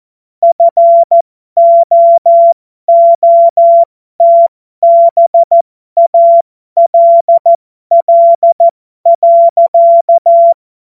Morse code, heard as a keyed tone, about 14 wpm, spaced as for 12 wpm, 680 Hz. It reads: FOOTBALL.
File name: Morsekurs / Witze